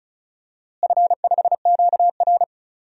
Morse code, heard as a keyed tone, about 36 wpm, 700 Hz. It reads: F5QR
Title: DXCC-100 Call Signs - 4 Characters - Repeated 2x - 35wpm